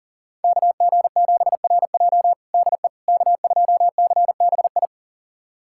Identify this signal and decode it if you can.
Morse code, heard as a keyed tone, about 40 wpm, 700 Hz. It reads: KG7RJ DE K2CBI
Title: USA Callsign DE Callsign - 5 Characters - 40wpm